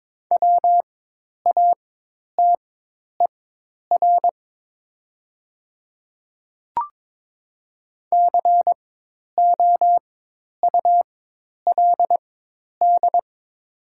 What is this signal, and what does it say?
Morse code, heard as a keyed tone, about 22 wpm, spaced as for 10 wpm, 700 Hz. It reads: WATER COULD
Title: 4x Character Spacing - Top 100 Words - 22wpm